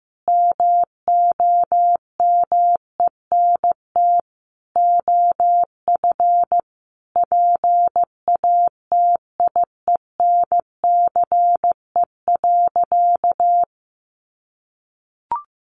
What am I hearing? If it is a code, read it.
Morse code, heard as a keyed tone, about 15 wpm, 700 Hz. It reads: MOMENT OF PATIENCE.
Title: Sentences from Top 1300 Words 15wpm